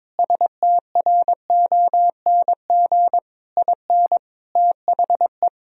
Morse code, heard as a keyed tone, about 21 wpm, 700 Hz. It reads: STRONG IN THE
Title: Sentences from Top 1000 Words - Review Part 4 - 22wpm